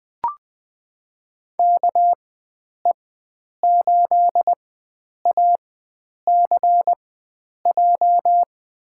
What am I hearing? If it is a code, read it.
Morse code, heard as a keyed tone, about 20 wpm, spaced as for 9 wpm, 700 Hz. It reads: KE8ACJ
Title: USA Call Signs - 6 Characters - 4x Character Spacing - 20wpm